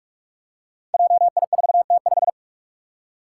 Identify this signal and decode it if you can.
Morse code, heard as a keyed tone, about 46 wpm, 700 Hz. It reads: JI4T5
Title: Five Challenging Letter-Number Combinations 45wpm